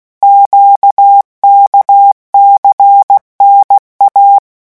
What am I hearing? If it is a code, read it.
Morse code, heard as a keyed tone, about 16 wpm, 800 Hz. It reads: QKCNA
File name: morse mp3 / revision5-vitesse 15 mots